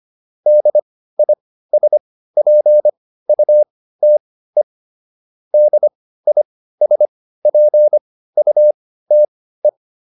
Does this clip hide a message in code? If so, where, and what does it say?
Yes: Morse code from 0.5 to 9.7 s, heard as a keyed tone, about 25 wpm, spaced as for 15 wpm, 600 Hz. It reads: DISPUTE DISPUTE